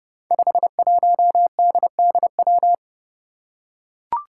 Morse code, heard as a keyed tone, about 29 wpm, 700 Hz. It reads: 51DDW
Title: Code Groups - 5 Letter:Numbers - 30wpm